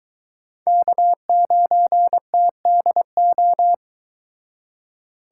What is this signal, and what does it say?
Morse code, heard as a keyed tone, about 23 wpm, 700 Hz. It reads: K9TDO